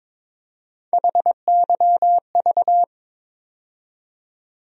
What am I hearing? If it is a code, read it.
Morse code, heard as a keyed tone, about 22 wpm, 700 Hz. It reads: HYV
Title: USA Call Sign Postfixes - 3 Letters 22wpm